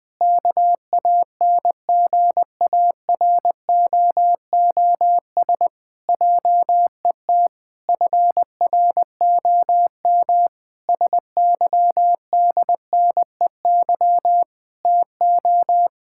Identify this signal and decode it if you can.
Morse code, heard as a keyed tone, about 20 wpm, 700 Hz. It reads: KANGAROOS JET FROM SYDNEY TO
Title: Pangram Challenge v2 - 20wpm